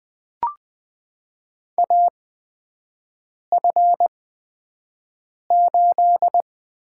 Morse code, heard as a keyed tone, about 20 wpm, spaced as for 5 wpm, 700 Hz. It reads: AF8
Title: USA Call Signs - 4 Characters - 8x Character Spacing - 20wpm